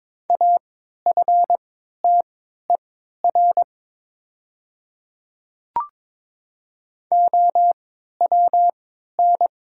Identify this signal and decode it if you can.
Morse code, heard as a keyed tone, about 21 wpm, spaced as for 12 wpm, 700 Hz. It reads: AFTER OWN
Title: 3x Character Spacing - Top 300 Words - 22wpm